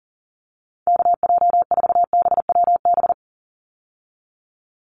Morse code, heard as a keyed tone, about 40 wpm, 700 Hz. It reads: KJ4BWB
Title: USA Call Signs 40wpm